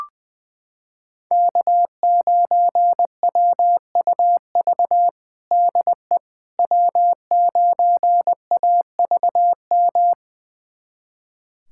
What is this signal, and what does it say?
Morse code, heard as a keyed tone, about 20 wpm, 700 Hz. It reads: K9WUV DE W9AVM
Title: USA Callsign DE Callsign - 5 Characters - 20wpm